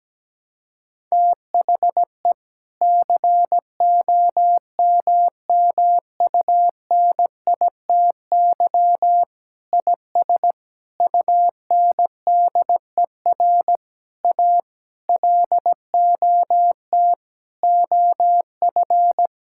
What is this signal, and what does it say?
Morse code, heard as a keyed tone, about 17 wpm, 700 Hz. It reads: THE COMMUNITY IS UNDER A LOT OF
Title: Sentences from Top 1000 Words - Review Part 6 - 17wpm